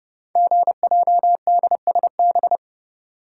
Morse code, heard as a keyed tone, about 28 wpm, 700 Hz. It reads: GJDSB
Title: Code Groups - 5 Letters - 30wpm